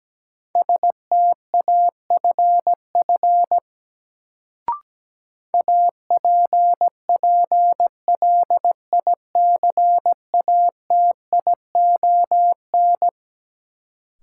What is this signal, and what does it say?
Morse code, heard as a keyed tone, about 17 wpm, 700 Hz. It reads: STAFF APPLICATION
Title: Sentences from Top 1200 Words - New Word Drill - 17wpm